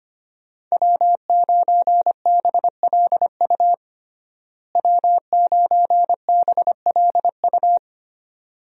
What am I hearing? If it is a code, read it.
Morse code, heard as a keyed tone, about 25 wpm, 700 Hz. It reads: W9BLU W9BLU
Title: USA Call Signs - 5 Characters - Repeated 2x - 25wpm